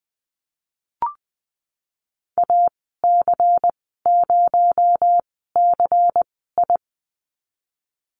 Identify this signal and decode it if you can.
Morse code, heard as a keyed tone, about 20 wpm, 700 Hz. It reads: A C 0 C I
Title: 231 Lesson 37g - Letter x - Call Signs 20wpm